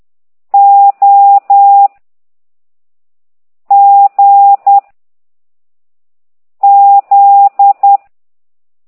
Morse code, heard as a keyed tone, about 10 wpm, 800 Hz. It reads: O G Z
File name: ct2idw / exerc11